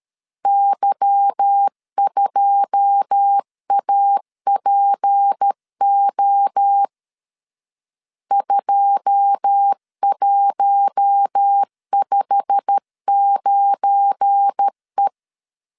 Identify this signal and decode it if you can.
Morse code, heard as a keyed tone, about 13 wpm, 800 Hz. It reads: Y2APO 2159E
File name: LEZIONI / 9 2 12wpm 2301